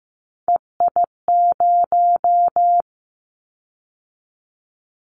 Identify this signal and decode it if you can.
Morse code, heard as a keyed tone, about 15 wpm, 700 Hz. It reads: EI0
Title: Callsign Prefixes for Top 50 DXCC - 15wpm